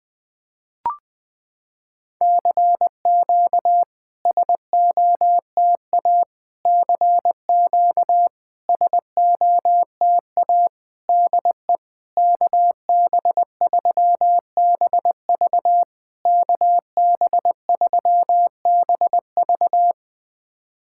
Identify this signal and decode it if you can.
Morse code, heard as a keyed tone, about 20 wpm, 700 Hz. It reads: CQ SOTA CQ SOTA DE KB3BV KB3BV
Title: SOTA QSO Practice 20wpm